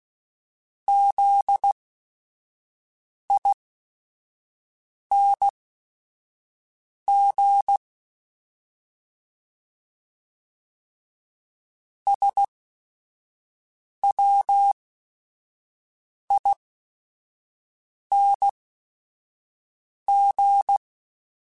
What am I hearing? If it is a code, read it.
Morse code, heard as a keyed tone, about 16 wpm, spaced as for 5 wpm, 800 Hz. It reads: ZING SWING